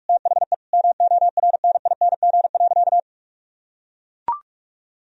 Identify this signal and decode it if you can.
Morse code, heard as a keyed tone, about 41 wpm, 700 Hz. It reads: THE MORNING.
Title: Sentences from Top 1000 Words - Review Part 2 - 45wpm